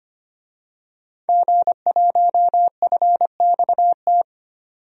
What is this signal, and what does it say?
Morse code, heard as a keyed tone, about 25 wpm, 700 Hz. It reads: G1FXT
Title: DXCC-100 Call Signs - 5 Characters - 25wpm